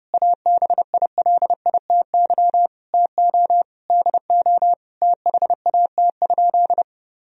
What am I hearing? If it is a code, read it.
Morse code, heard as a keyed tone, about 30 wpm, 700 Hz. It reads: ABILITY TO DO THAT?
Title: Sentences from Top 1000 Words - Review Part 3 - 30wpm